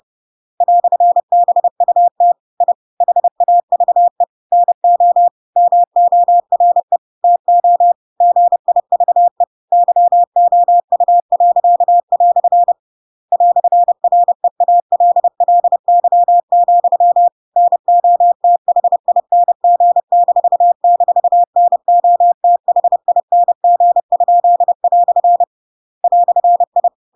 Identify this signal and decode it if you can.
Morse code, heard as a keyed tone, about 30 wpm, 700 Hz. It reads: "BUT I HAVE NO MORE TO GIVE YOU." "REALLY, NOTHING--NOTHING?" "I
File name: Pinocchio / Pinocchio ch7 30 30 ik1hge